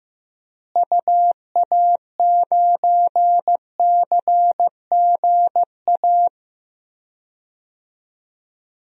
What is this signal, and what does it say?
Morse code, heard as a keyed tone, about 15 wpm, 700 Hz. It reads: UA9CGA
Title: DXCC-100 Call Signs - 6 Characters - Repeated 3x - 15wpm